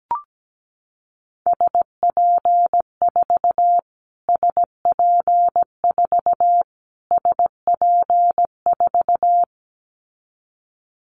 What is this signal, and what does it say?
Morse code, heard as a keyed tone, about 17 wpm, 700 Hz. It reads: SP4 SP4 SP4
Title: Callsign Prefixes for Top 25 DXCC - Repeated 3x - 17wpm